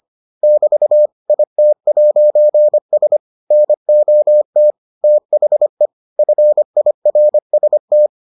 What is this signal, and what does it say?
Morse code, heard as a keyed tone, about 25 wpm, 600 Hz. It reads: <BT> IT'S NOT THE FIRST